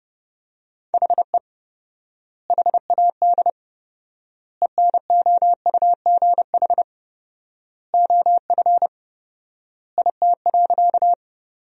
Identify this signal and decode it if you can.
Morse code, heard as a keyed tone, about 32 wpm, 700 Hz. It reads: HE HAD ENOUGH OF IT.
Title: 4x Word-Spacing - Sentences from Top 400 Words - 30wpm